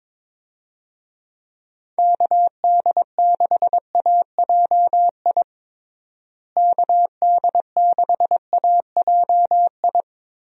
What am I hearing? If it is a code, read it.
Morse code, heard as a keyed tone, about 22 wpm, 700 Hz. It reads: KD6AJI KD6AJI
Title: USA Call Signs - 6 Characters - Repeated 2x - 22wpm